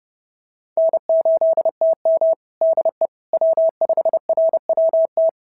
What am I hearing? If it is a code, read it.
Morse code, heard as a keyed tone, about 30 wpm, 650 Hz. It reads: N8TM DE W5RWT
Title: SOTA QSO Practice 30wpm